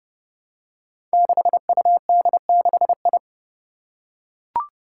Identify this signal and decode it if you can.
Morse code, heard as a keyed tone, about 30 wpm, 700 Hz. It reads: 6UD6I
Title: Five Challenging Letter-Number Combinations 30wpm